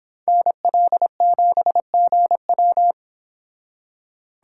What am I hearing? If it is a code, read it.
Morse code, heard as a keyed tone, about 25 wpm, 700 Hz. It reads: NL7GW